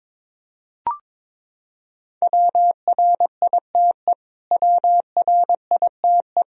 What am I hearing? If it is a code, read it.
Morse code, heard as a keyed tone, about 21 wpm, 700 Hz. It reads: WRITE WRITE
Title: Top 100 Words Repeated 22wpm